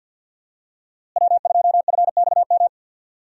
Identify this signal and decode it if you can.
Morse code, heard as a keyed tone, about 52 wpm, 700 Hz. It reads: W2FXM
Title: USA Call Signs - 5 Characters - Repeated 2x - 50wpm